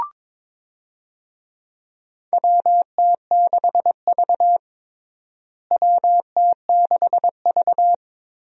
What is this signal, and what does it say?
Morse code, heard as a keyed tone, about 22 wpm, 700 Hz. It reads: WT6V WT6V
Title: USA Call Signs - 4 Characters - Repeated 3x - 22wpm